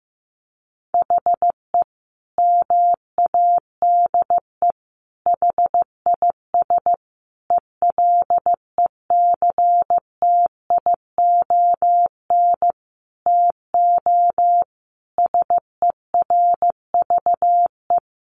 Morse code, heard as a keyed tone, about 14 wpm, 700 Hz. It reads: HE MADE HIS ELECTION TO SERVE
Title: Sentences from Top 1500 Words 15wpm